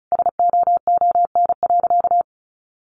Morse code, heard as a keyed tone, about 34 wpm, 700 Hz. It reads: SOON.
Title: Sentences from Top 1500 Words 35wpm